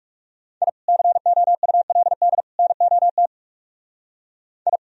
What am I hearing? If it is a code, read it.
Morse code, heard as a keyed tone, about 39 wpm, 700 Hz. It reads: I COULD NOT I